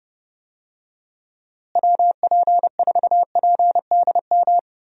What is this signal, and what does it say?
Morse code, heard as a keyed tone, about 30 wpm, 700 Hz. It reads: WP4PDM
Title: USA Call Signs - 6 Characters - Repeated 3x - 30wpm